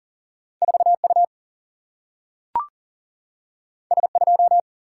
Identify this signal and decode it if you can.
Morse code, heard as a keyed tone, about 40 wpm, 700 Hz. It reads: 4 U S 2
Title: Single-Letter Single-Number 40wpm